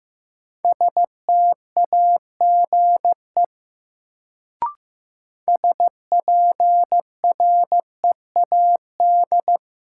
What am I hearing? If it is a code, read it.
Morse code, heard as a keyed tone, about 15 wpm, 700 Hz. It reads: STAGE SPREAD
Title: Sentences from Top 1200 Words - New Word Drill Alt - 15wpm